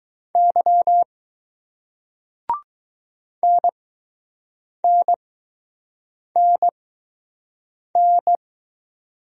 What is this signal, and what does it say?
Morse code, heard as a keyed tone, about 20 wpm, spaced as for 6 wpm, 700 Hz. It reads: Y NNNN